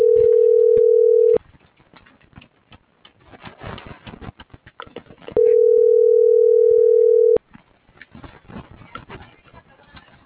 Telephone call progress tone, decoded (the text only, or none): ringback tone